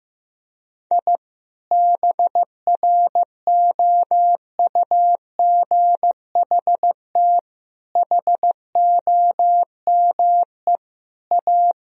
Morse code, heard as a keyed tone, about 15 wpm, 700 Hz. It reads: I BROUGHT HOME A
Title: Sentences from Top 1000 Words - Review Part 4 - 15wpm